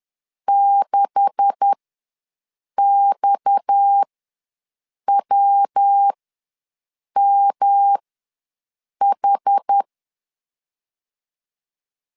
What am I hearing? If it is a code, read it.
Morse code, heard as a keyed tone, about 10 wpm, spaced as for 6 wpm, 800 Hz. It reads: 6XWMH
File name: LEZIONI / 9 286 ws 10wpm 4032